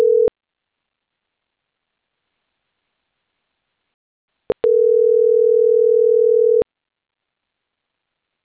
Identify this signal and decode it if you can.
Ringback tone (North America), 440 + 480 Hz.